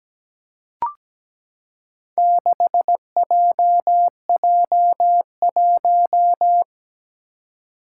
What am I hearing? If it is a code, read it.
Morse code, heard as a keyed tone, about 17 wpm, 700 Hz. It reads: 6JJ1
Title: Four Challenging Letter-Number Combinations 17wpm